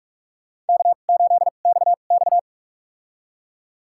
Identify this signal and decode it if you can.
Morse code, heard as a keyed tone, about 45 wpm, spaced as for 32 wpm, 700 Hz. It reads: K8XX
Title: USA Call Signs - 2x Character Spacing - 45wpm